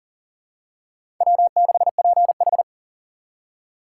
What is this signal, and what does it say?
Morse code, heard as a keyed tone, about 37 wpm, 700 Hz. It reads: W6PH
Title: USA Call Signs - 4 Characters - 3x Character Spacing - 40wpm